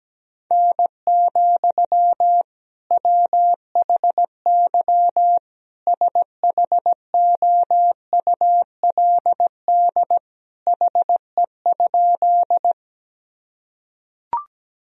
Morse code, heard as a keyed tone, about 17 wpm, 700 Hz. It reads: N, WHY SHOULD HE?
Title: Sentences from Top 300 Words and Abbreviations 17wpm